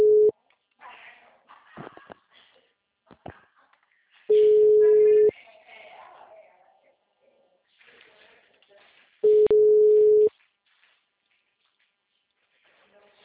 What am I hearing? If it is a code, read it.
Ringback tone (Europe, 425 Hz), with speech.